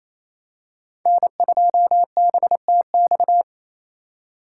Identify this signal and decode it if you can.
Morse code, heard as a keyed tone, about 28 wpm, 700 Hz. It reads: N2BTX
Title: K1USN SST QSO Practice - 28wpm